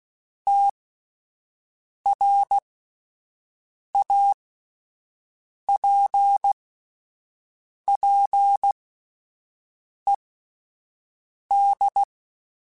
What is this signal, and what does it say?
Morse code, heard as a keyed tone, about 16 wpm, spaced as for 5 wpm, 800 Hz. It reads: TRAPPED
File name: (spaced) / QSOs Spaced-mp3s / WA7QOX 2 @10 sp